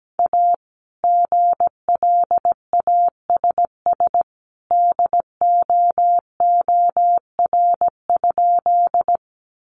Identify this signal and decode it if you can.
Morse code, heard as a keyed tone, about 17 wpm, 700 Hz. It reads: A GLASS DOOR?
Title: Sentences from Top 1300 Words 17wpm